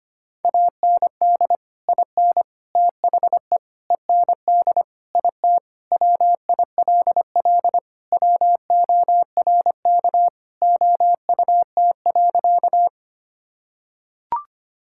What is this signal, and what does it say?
Morse code, heard as a keyed tone, about 25 wpm, 700 Hz. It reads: AND IN THE END IT WILL WORK OUT.
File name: practice / 25wpm / Sentences with 3-Word Phrases - 25wpm